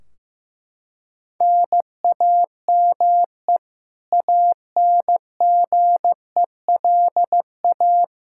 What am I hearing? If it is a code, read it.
Morse code, heard as a keyed tone, about 15 wpm, 700 Hz. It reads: NAME ANGELA
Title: Practice QSO - Exchange 1 - Beginning - 15wpm